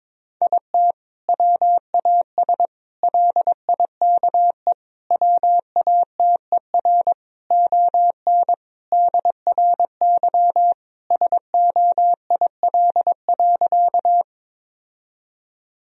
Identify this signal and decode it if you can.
Morse code, heard as a keyed tone, about 22 wpm, 700 Hz. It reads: IT WAS LIKE WATER ON DRY SOIL.
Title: Sentences from Top 1000 Words - Review Part 4 - 22wpm